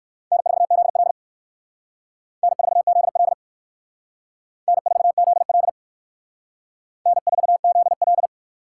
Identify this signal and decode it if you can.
Morse code, heard as a keyed tone, about 56 wpm, 700 Hz. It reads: N4ZL N4ZL N4ZL N4ZL